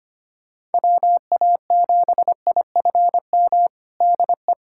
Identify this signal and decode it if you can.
Morse code, heard as a keyed tone, about 25 wpm, 700 Hz. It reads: WA7IFM DE
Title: USA Callsign DE Callsign - 6 Characters - 25wpm